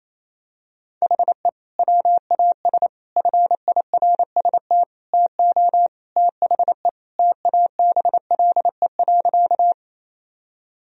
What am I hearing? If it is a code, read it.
Morse code, heard as a keyed tone, about 28 wpm, 700 Hz. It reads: HE WAS FIRST TO THE TABLE.
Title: Sentences from Top 1000 Words - Review Part 2 - 28wpm